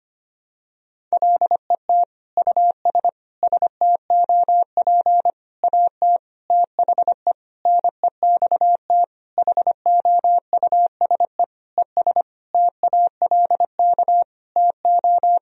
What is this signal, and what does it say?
Morse code, heard as a keyed tone, about 25 wpm, 700 Hz. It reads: LET US STOP AT THE NEXT HOUSE ES TALK TO
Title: Sentences from Top 400 Words and Abbreviations 25wpm